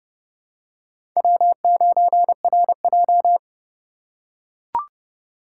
Morse code, heard as a keyed tone, about 30 wpm, 690 Hz. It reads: W9RJ
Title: USA Call Signs - 4 Characters - 2x Character Spacing - 30wpm